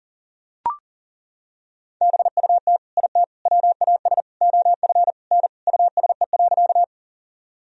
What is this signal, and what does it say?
Morse code, heard as a keyed tone, about 40 wpm, 700 Hz. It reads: BUT IT WAS OF N USE.
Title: Sentences from Top 100 Words and Abbreviations 40wpm